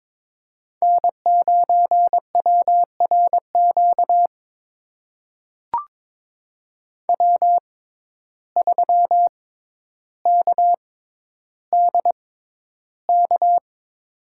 Morse code, heard as a keyed tone, about 22 wpm, spaced as for 11 wpm, 700 Hz. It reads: N9WRQ W3KDK
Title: USA Call Signs - 5 Characters - 6x Character Spacing - 22wpm